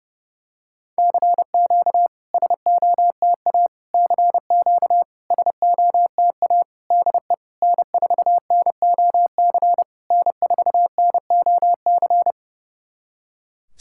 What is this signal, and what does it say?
Morse code, heard as a keyed tone, about 32 wpm, 700 Hz. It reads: CQ SOTA CQ SOTA DE N4NOC N4NOC
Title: SOTA QSO Practice 30wpm